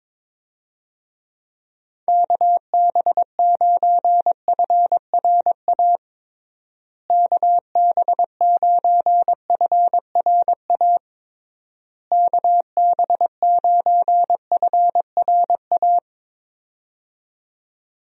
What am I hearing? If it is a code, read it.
Morse code, heard as a keyed tone, about 22 wpm, 700 Hz. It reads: KB9FRA KB9FRA KB9FRA
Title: USA Call Signs - 6 Characters - Repeated 3x - 22wpm